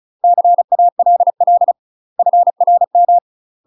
Morse code, heard as a keyed tone, about 36 wpm, 700 Hz. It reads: CALL FRM
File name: UK MASTER QSOs TRAINING / 35WPM / UK MASTER QSOs 0012